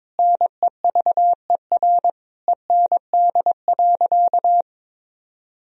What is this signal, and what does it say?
Morse code, heard as a keyed tone, about 22 wpm, 700 Hz. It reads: NEVER END.
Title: Sentences from Top 1000 Words - Review Part 3 - 22wpm